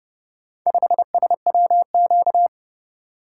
Morse code, heard as a keyed tone, about 31 wpm, 700 Hz. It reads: 5SWQ